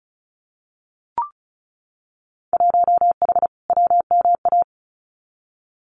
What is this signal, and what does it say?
Morse code, heard as a keyed tone, about 36 wpm, 700 Hz. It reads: 1H WMA
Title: Winter Field Day Exchanges 35wpm